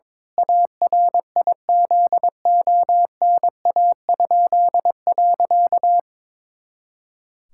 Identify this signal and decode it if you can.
Morse code, heard as a keyed tone, about 21 wpm, 700 Hz. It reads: ARIZONA?.
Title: Pangram Challenge v2 - 22wpm